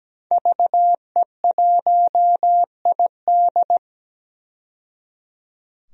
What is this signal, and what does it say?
Morse code, heard as a keyed tone, about 17 wpm, 700 Hz. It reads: VE1ID
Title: DXCC-100 Call Signs - 5 Characters - 17wpm